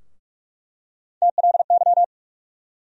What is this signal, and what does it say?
Morse code, heard as a keyed tone, about 46 wpm, 700 Hz. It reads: TLY
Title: USA Call Sign Postfixes - 3 Letters 45wpm